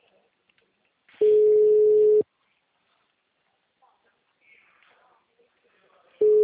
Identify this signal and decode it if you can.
Ringback tone (Europe, 425 Hz), with speech.